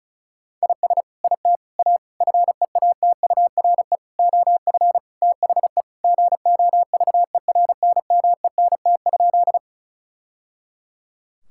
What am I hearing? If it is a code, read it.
Morse code, heard as a keyed tone, about 36 wpm, 700 Hz. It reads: IS IT A FEATURE OF THE GOVERNMENT?
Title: Sentences from Top 1000 Words - Review Part 5 - 35wpm